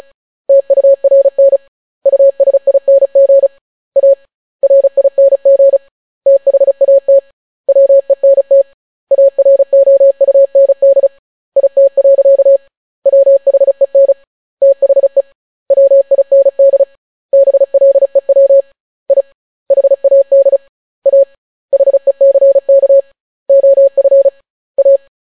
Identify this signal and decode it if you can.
Morse code, heard as a keyed tone, about 36 wpm, 550 Hz. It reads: TURN USING A RING THAT WENT AROUND IT. WHEN THE WIND BLEW I HAD A HECK OF A